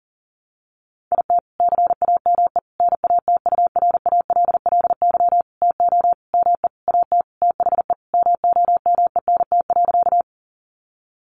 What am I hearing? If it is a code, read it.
Morse code, heard as a keyed tone, about 40 wpm, 700 Hz. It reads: IT CAME NATURALLY TO ME AT THE MOMENT.
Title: Sentences from Top 1300 Words 40wpm